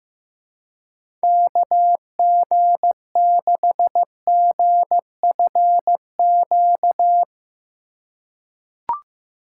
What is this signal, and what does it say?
Morse code, heard as a keyed tone, about 15 wpm, 700 Hz. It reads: KG6GFQ
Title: USA Call Signs - 6 Characters - 3x Character Spacing - 15wpm